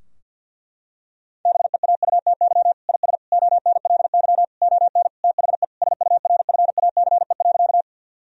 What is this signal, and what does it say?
Morse code, heard as a keyed tone, about 50 wpm, 700 Hz. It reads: BEAUTY IS ONLY ON THE SURFACE.
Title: Sentences from Top 1000 Words - Review Part 3 - 50wpm